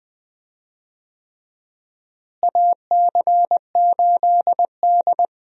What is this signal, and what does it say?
Morse code, heard as a keyed tone, about 20 wpm, 700 Hz. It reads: AC8D
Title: USA Call Signs - 4 Characters - Repeated 3x - 20wpm